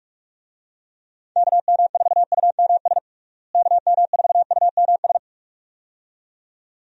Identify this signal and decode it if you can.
Morse code, heard as a keyed tone, about 46 wpm, 700 Hz. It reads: KM4UMS KM4UMS
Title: USA Call Signs - 6 Characters - Repeated 2x - 45wpm